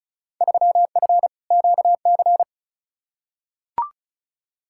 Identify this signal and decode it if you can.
Morse code, heard as a keyed tone, about 36 wpm, 700 Hz. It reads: 3F QC
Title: Field Day Exchanges - 35wpm